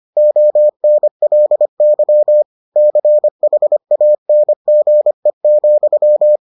Morse code, heard as a keyed tone, about 25 wpm, 600 Hz. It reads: ONLY CHANGE,